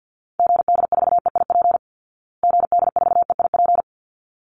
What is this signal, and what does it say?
Morse code, heard as a keyed tone, about 50 wpm, 700 Hz. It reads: GD4EIP GD4EIP
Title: Call Signs Repeated 3x - 50wpm